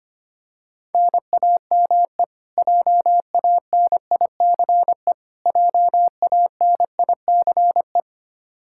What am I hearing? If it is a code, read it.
Morse code, heard as a keyed tone, about 25 wpm, 700 Hz. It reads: NAME JANICE JANICE
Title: Practice QSO - Exchange 1 - Beginning - 25wpm